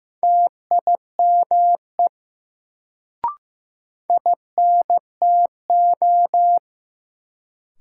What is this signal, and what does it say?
Morse code, heard as a keyed tone, about 15 wpm, 700 Hz. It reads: TIME INTO
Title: Sets of 2 Words - New Word Drill - 15wpm